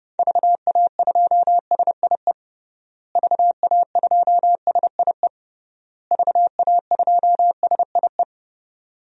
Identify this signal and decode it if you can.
Morse code, heard as a keyed tone, about 30 wpm, 700 Hz. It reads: VA2SIE VA2SIE VA2SIE
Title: DXCC-100 Call Signs - 6 Characters - Repeated 3x - 30wpm